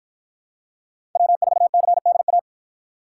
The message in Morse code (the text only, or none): W4CDA